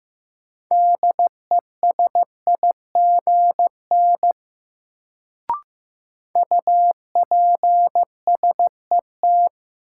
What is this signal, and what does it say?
Morse code, heard as a keyed tone, about 15 wpm, 700 Hz. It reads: DESIGN UPSET
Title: Sentences from Top 800 Words - New Word Drill - 15wpm